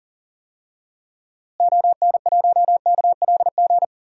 Morse code, heard as a keyed tone, about 40 wpm, 700 Hz. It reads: ON1KLG
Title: DXCC-100 Call Signs - 6 Characters - Repeated 3x - 40wpm